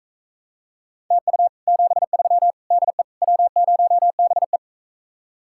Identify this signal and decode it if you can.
Morse code, heard as a keyed tone, about 45 wpm, 700 Hz. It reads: TU 73 DE W0BE